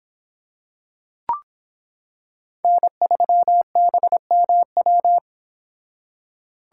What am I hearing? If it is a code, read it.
Morse code, heard as a keyed tone, about 26 wpm, 700 Hz. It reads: N3BMW